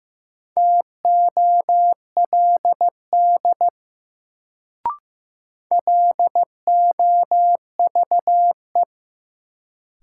Morse code, heard as a keyed tone, about 15 wpm, 700 Hz. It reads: TOLD LOVE
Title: Sentences from Top 400 Words - New Word Drill - 15wpm